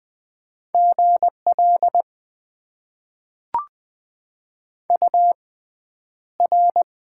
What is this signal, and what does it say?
Morse code, heard as a keyed tone, about 20 wpm, spaced as for 10 wpm, 700 Hz. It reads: GL UR
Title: Top 90 QSO Elements - 6x Character Spacing - 20wpm